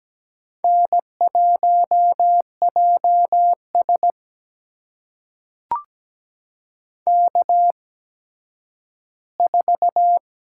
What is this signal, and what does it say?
Morse code, heard as a keyed tone, about 17 wpm, spaced as for 10 wpm, 700 Hz. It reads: N1JS K4
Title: USA Call Signs - 8x Character Spacing - 17wpm